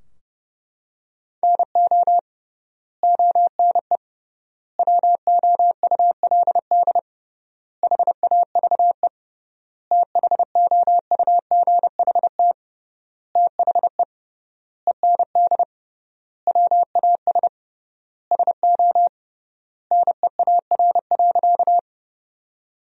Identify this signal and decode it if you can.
Morse code, heard as a keyed tone, about 32 wpm, 700 Hz. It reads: NO ONE WOULD HAVE THOUGHT THE END WAS SO NEAR.
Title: 3x Word-Spacing - Sentences from Top 300 Words - 30wpm